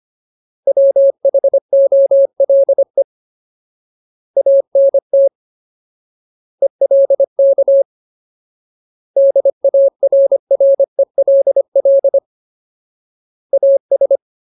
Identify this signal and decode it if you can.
Morse code, heard as a keyed tone, about 25 wpm, 550 Hz. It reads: WHOLE ANT ELK DARRELL AS
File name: Ultimate Word List / 25WPM Ultimate